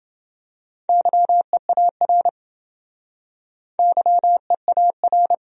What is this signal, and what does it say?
Morse code, heard as a keyed tone, about 28 wpm, 700 Hz. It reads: YEAR YEAR